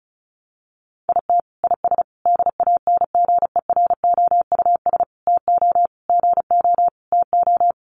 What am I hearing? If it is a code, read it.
Morse code, heard as a keyed tone, about 34 wpm, 700 Hz. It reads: IT IS DANGEROUS TO GO TO
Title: Sentences from Top 1200 Words 35wpm